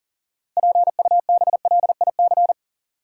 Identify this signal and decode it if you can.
Morse code, heard as a keyed tone, about 40 wpm, 700 Hz. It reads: PUBLIC
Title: Sets of 2 Words - New Word Drill - 40wpm